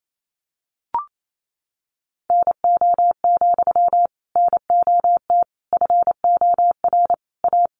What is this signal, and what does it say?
Morse code, heard as a keyed tone, about 28 wpm, 700 Hz. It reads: NO, NOT FOR A
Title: Sentences from Top 1000 Words 28wpm